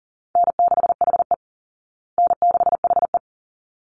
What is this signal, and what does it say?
Morse code, heard as a keyed tone, about 40 wpm, 700 Hz. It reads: N6HE N6HE
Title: Easy Call Signs Repeated 2x - 40wpm